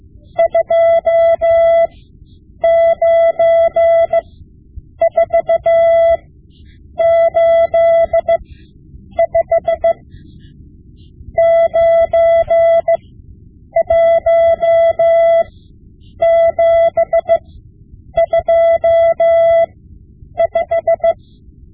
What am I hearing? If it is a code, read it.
Morse code, heard as a keyed tone, about 16 wpm, spaced as for 8 wpm, 670 Hz. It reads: 29485 91725
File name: Morse Lessons / Morse Lessons / Tape-5B